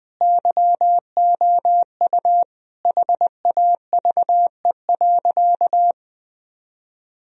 Morse code, heard as a keyed tone, about 20 wpm, 700 Hz. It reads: YOU HAVE.
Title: Sentences from Top 1000 Words - Review Part 3 - 20wpm